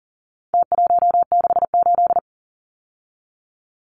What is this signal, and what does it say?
Morse code, heard as a keyed tone, about 40 wpm, 700 Hz. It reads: T168
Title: Numbers - Contest Style - 4-Digit 40wpm